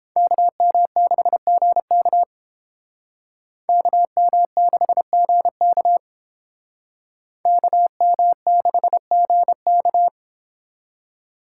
Morse code, heard as a keyed tone, about 31 wpm, 700 Hz. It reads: KM6GK KM6GK KM6GK